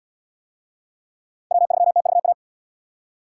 Morse code, heard as a keyed tone, about 64 wpm, 700 Hz. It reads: W3TVA